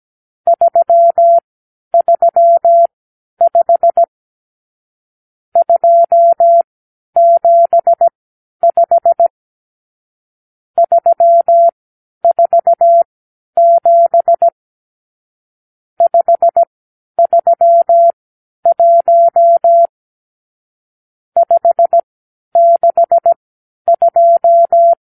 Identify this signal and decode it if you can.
Morse code, heard as a keyed tone, about 17 wpm, spaced as for 10 wpm, 680 Hz. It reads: 335 275 347 531 562
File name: k7qo / Lesson 52 (8)